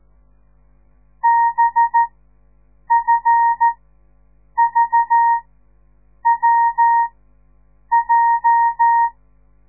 Morse code, heard as a keyed tone, about 13 wpm, spaced as for 7 wpm, 930 Hz. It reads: BFVWJ
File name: radiomorningstar / morse / les 9